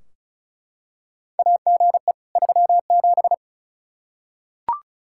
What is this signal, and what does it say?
Morse code, heard as a keyed tone, about 36 wpm, 700 Hz. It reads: AGE 37
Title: K1USN SST QSO Practice 35wpm